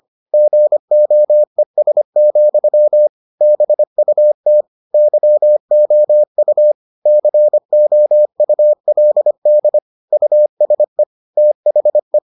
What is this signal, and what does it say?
Morse code, heard as a keyed tone, about 25 wpm, 600 Hz. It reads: GOES, BUT YOU COULD USE THE